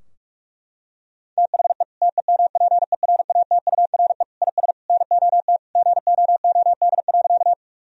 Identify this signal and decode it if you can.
Morse code, heard as a keyed tone, about 41 wpm, 700 Hz. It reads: THE TEMPERATURE IS NOT GOOD.
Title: Sentences from Top 1000 Words - Review Part 4 - 45wpm